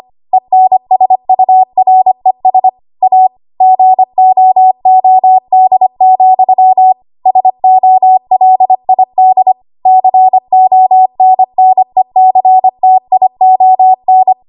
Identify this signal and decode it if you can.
Morse code, heard as a keyed tone, about 25 wpm, 750 Hz. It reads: ENSURES A GOOD, SOLID CONNECTION
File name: Archive / 25 WPM / 25WPM